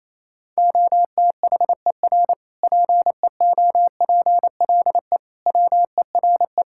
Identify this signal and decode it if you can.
Morse code, heard as a keyed tone, about 28 wpm, 700 Hz. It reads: OTHER PEOPLE WERE
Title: Sentences from Top 1000 Words - Review Part 5 - 28wpm